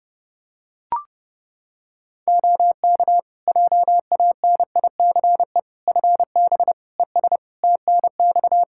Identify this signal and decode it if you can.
Morse code, heard as a keyed tone, about 30 wpm, 700 Hz. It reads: OK JANICE FB ES TNX
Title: K1USN SST QSO Practice 30wpm